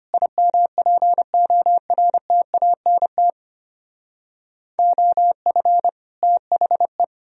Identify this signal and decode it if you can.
Morse code, heard as a keyed tone, about 33 wpm, 700 Hz. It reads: IMPORTANT OF THE